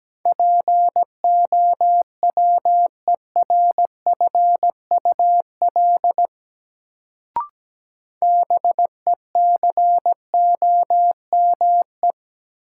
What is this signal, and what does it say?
Morse code, heard as a keyed tone, about 17 wpm, 700 Hz. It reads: POWERFUL BECOME
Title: Sets of 5 Words Encore - New Word Drill - 17wpm